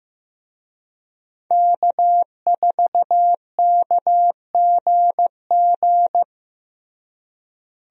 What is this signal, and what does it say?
Morse code, heard as a keyed tone, about 15 wpm, 700 Hz. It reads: K4KGG
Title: USA Call Signs - 5 Characters - Repeated 2x - 15wpm